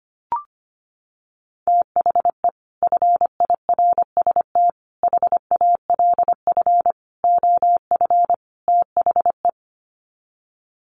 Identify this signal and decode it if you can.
Morse code, heard as a keyed tone, about 25 wpm, 700 Hz. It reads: THE FIRST HALF OF THE
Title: Sets of 5 Words 25wpm